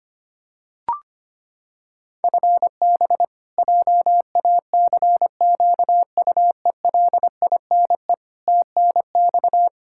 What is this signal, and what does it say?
Morse code, heard as a keyed tone, about 25 wpm, 700 Hz. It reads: FB JACQUELINE TNX
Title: Practice QSO - Exchange 3 - More Info - 25wpm